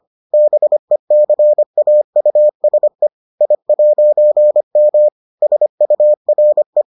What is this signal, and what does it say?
Morse code, heard as a keyed tone, about 25 wpm, 600 Hz. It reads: BECAUSE I'M SURE